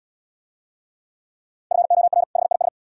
Morse code, heard as a keyed tone, about 69 wpm, 700 Hz. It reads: FOR HIS